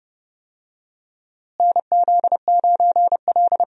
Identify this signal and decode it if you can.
Morse code, heard as a keyed tone, about 30 wpm, 700 Hz. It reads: NZ9L
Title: USA Call Signs - 4 Characters - Repeated 3x - 30wpm